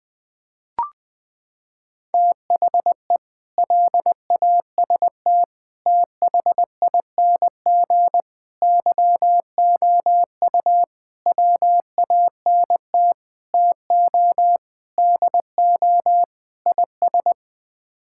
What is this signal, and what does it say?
Morse code, heard as a keyed tone, about 21 wpm, 700 Hz. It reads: THE LAST THING YOU WANT TO DO IS
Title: Sets of 8 Words - 20wpm